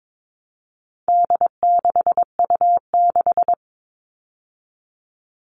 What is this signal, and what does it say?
Morse code, heard as a keyed tone, about 22 wpm, 700 Hz. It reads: D6U6
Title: All Cut Numbers - 4-Digit 22wpm